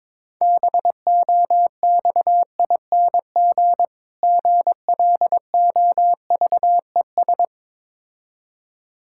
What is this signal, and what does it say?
Morse code, heard as a keyed tone, about 22 wpm, 690 Hz. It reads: BOXING GLOVES
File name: Difficult Words with Hints - 22wpm